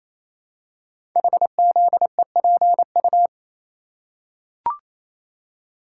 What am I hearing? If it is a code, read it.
Morse code, heard as a keyed tone, about 28 wpm, 700 Hz. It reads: HZEPU
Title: Code Groups - 5 Letters - 28wpm